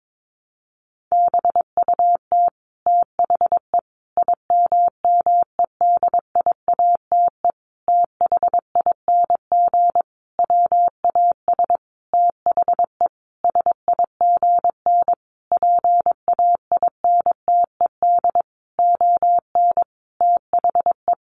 Morse code, heard as a keyed tone, about 22 wpm, 700 Hz. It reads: BUT THE IMMEDIATE THING WAS THE SIGN PAINTED ON THE
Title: A Sound Of Thunder 22wpm